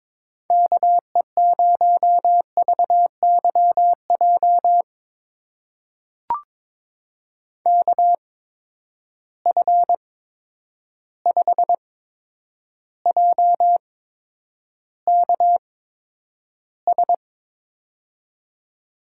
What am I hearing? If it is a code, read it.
Morse code, heard as a keyed tone, about 22 wpm, spaced as for 9 wpm, 700 Hz. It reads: KE0VYJ KF5JKS
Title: USA Call Signs - 6 Characters - 8x Character Spacing - 22wpm